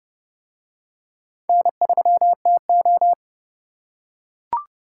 Morse code, heard as a keyed tone, about 27 wpm, 700 Hz. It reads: N3TO